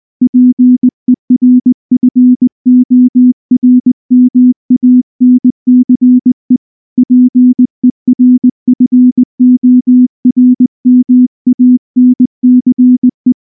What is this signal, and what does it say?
Morse code, heard as a keyed tone, about 20 wpm, 250 Hz. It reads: PERFORMANCE PERFORMANCE
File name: morse / morse-training-20WPM-2